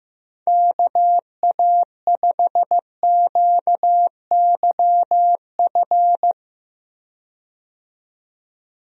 Morse code, heard as a keyed tone, about 15 wpm, 700 Hz. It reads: KA5QYF
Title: USA Call Signs - 6 Characters - 15wpm